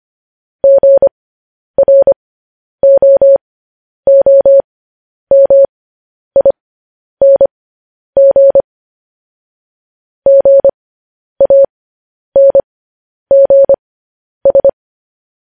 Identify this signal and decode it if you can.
Morse code, heard as a keyed tone, about 25 wpm, spaced as for 10 wpm, 550 Hz. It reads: GROOMING GANGS
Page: news-headlines-in-morse-code-at-10-wpm